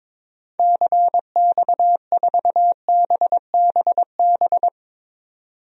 Morse code, heard as a keyed tone, about 22 wpm, 700 Hz. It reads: CX4BBB
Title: DXCC-100 Call Signs - 6 Characters - 22wpm